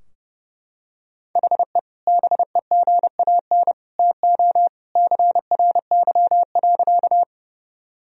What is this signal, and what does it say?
Morse code, heard as a keyed tone, about 29 wpm, 700 Hz. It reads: HE BEGAN TO CRY.
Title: Sentences from Top 1000 Words - Review Part 2 - 30wpm